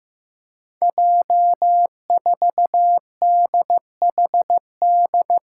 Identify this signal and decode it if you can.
Morse code, heard as a keyed tone, about 15 wpm, 700 Hz. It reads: J4DHD
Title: Five Challenging Letter-Number Combinations 15wpm